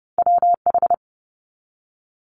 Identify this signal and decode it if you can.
Morse code, heard as a keyed tone, about 30 wpm, 700 Hz. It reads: WH
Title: Multi-letter Phonemes 30wpm